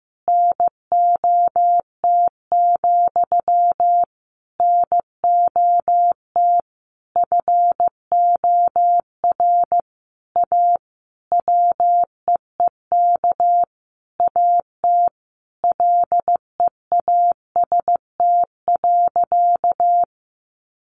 Morse code, heard as a keyed tone, about 15 wpm, 700 Hz. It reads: NOT, NOT FOR A WEEK AT LEAST.
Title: Sentences from Top 900 Words 15wpm